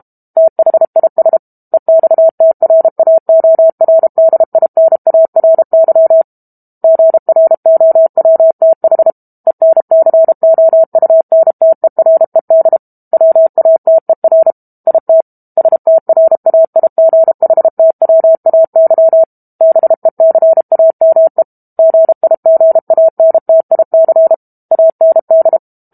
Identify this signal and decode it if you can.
Morse code, heard as a keyed tone, about 32 wpm, 650 Hz. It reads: THIS EXTRAORDINARY GROWTH ENCOUNTERED WATER IT STRAIGHTWAY BECAME GIGANTIC AND
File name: Book Two / 06-Chapter 6 - 32 WPM